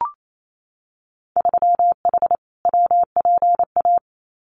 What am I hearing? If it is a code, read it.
Morse code, heard as a keyed tone, about 28 wpm, 700 Hz. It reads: 3H WPA
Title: Winter Field Day Exchanges 28wpm